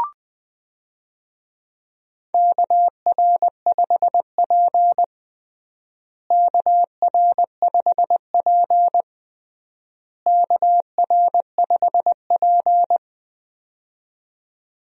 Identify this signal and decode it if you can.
Morse code, heard as a keyed tone, about 20 wpm, 700 Hz. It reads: KR5P KR5P KR5P
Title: USA Call Signs - 4 Characters - Repeated 3x - 20wpm